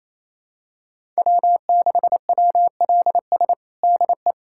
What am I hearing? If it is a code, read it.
Morse code, heard as a keyed tone, about 28 wpm, 700 Hz. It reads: W6WLS DE
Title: USA Callsign DE Callsign - 5 Characters - 28wpm